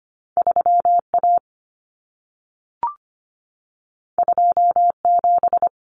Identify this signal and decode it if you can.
Morse code, heard as a keyed tone, about 25 wpm, 700 Hz. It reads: 3A 27
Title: Common Cut Numbers - 2-Digit 25wpm